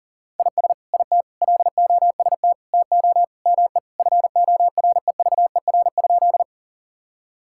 Morse code, heard as a keyed tone, about 40 wpm, 700 Hz. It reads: IS IT LOST TO ME FOREVER?
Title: Sentences from Top 1600 Words 40wpm